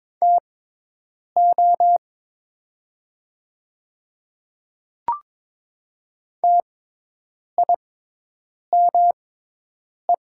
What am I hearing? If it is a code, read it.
Morse code, heard as a keyed tone, about 22 wpm, spaced as for 8 wpm, 700 Hz. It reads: TO TIME
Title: 6x Character Spacing - Top 100 Words - 22wpm